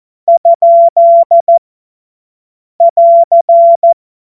Morse code, heard as a keyed tone, about 14 wpm, 680 Hz. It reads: ? <AR>
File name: Morsekurs / Witze / Witz 14. März 2023 50 F70